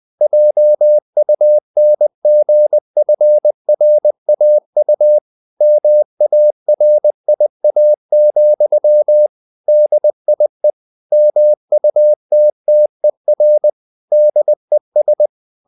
Morse code, heard as a keyed tone, about 20 wpm, 600 Hz. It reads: JUNGFRAU MARIA, DIE MUTTER DES